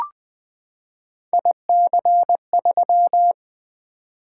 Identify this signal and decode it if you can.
Morse code, heard as a keyed tone, about 20 wpm, 700 Hz. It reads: IC3
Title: Three Letter-Number Combinations - 20wpm